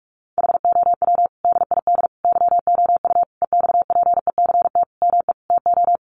Morse code, heard as a keyed tone, about 46 wpm, 700 Hz. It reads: HOW DID YOU EXPECT ME TO
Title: Sentences from Top 1100 Words 45wpm